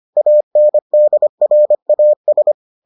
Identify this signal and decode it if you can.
Morse code, heard as a keyed tone, about 25 wpm, 600 Hz. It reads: ANDRAS